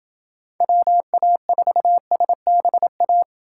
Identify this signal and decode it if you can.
Morse code, heard as a keyed tone, about 29 wpm, 700 Hz. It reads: WA4SBA